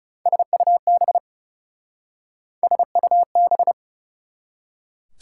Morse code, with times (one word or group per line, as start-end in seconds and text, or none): 0.3-1.2 SUB
2.6-3.7 SUB